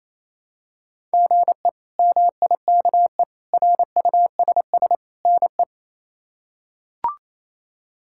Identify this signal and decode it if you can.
Morse code, heard as a keyed tone, about 28 wpm, 700 Hz. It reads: GE MIKE RUSS NE
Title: K1USN SST QSO Practice - 28wpm